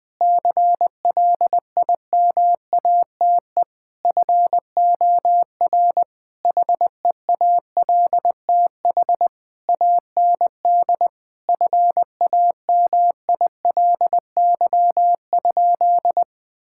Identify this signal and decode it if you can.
Morse code, heard as a keyed tone, about 21 wpm, 700 Hz. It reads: CLIMATE FOR HEALTH AND FAMILY?
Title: Sentences from Top 1600 Words 20wpm